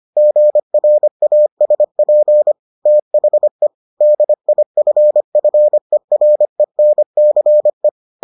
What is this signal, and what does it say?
Morse code, heard as a keyed tone, about 25 wpm, 600 Hz. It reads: GRASP THE DIFFERENCE